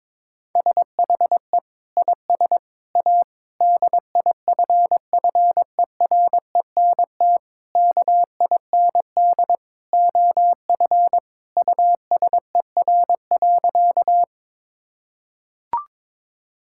Morse code, heard as a keyed tone, about 22 wpm, 700 Hz. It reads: SHE IS A DIFFERENT KIND OF USER.
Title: Sentences from Top 1000 Words - Review Part 5 - 22wpm